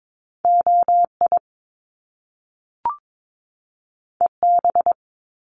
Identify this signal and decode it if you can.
Morse code, heard as a keyed tone, about 22 wpm, 700 Hz. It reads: OI EB
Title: Top Two Letter Combinations in Words 22wpm